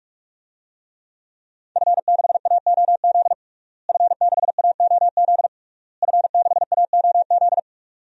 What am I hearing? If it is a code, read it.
Morse code, heard as a keyed tone, about 42 wpm, 700 Hz. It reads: F6AOZ F6AOZ F6AOZ
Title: DXCC-100 Call Signs - 5 Characters - Repeated 3x - 45wpm